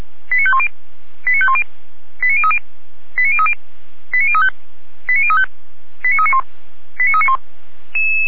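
Five-tone selective call (ZVEI), heard as quick runs of five tones, tones of about 70 ms. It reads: 89510 89510 89030 89030 89035 89035 89391 89391
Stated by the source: Received on SDR, ZVEI five-tone call, address 89042.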